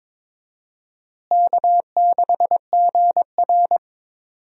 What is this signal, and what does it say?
Morse code, heard as a keyed tone, about 22 wpm, 700 Hz. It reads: K6GR
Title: USA Call Signs - 4 Characters - Repeated 3x - 22wpm